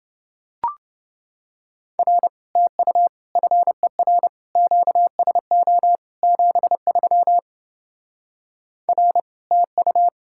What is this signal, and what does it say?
Morse code, heard as a keyed tone, about 28 wpm, 700 Hz. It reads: R TU FER QSO 73 R TU